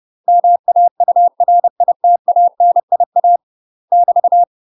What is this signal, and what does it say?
Morse code, heard as a keyed tone, about 30 wpm, 700 Hz. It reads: MAURITANIA <BT>
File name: UK MASTER QSOs TRAINING / 30WPM / UK MASTER QSOs 0024